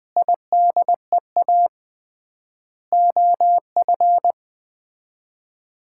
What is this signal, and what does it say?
Morse code, heard as a keyed tone, about 20 wpm, 700 Hz. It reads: IDEA OF
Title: 3x Word-Spacing - Sets of 3 Words - 20wpm